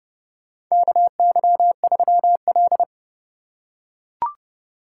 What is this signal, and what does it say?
Morse code, heard as a keyed tone, about 29 wpm, 700 Hz. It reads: KY3L
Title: USA Call Signs - 4 Characters - 4x Character Spacing - 30wpm